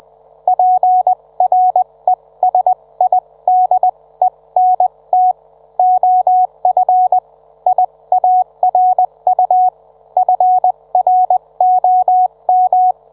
Morse code, heard as a keyed tone, about 20 wpm, spaced as for 17 wpm, 740 Hz. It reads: PRESIDENT OF IARU FROM